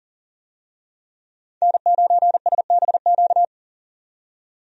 Morse code, heard as a keyed tone, about 40 wpm, 700 Hz. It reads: N9SBQ
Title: USA Call Signs - 5 Characters - 40wpm